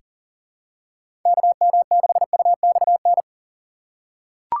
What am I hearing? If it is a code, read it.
Morse code, heard as a keyed tone, about 40 wpm, 700 Hz. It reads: KM6UXN
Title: USA Call Signs - 6 Characters - 6x Character Spacing - 40wpm